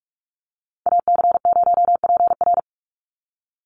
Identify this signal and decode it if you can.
Morse code, heard as a keyed tone, about 44 wpm, 700 Hz. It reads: AC0PR
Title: NA SOTA Activator Callsigns 45wpm